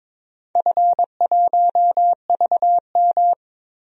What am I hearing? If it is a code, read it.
Morse code, heard as a keyed tone, about 22 wpm, 700 Hz. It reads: F1VM
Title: DXCC-100 Call Signs - 4 Characters - 22wpm